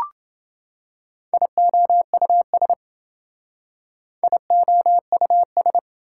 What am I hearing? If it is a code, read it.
Morse code, heard as a keyed tone, about 28 wpm, 700 Hz. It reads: IOUS IOUS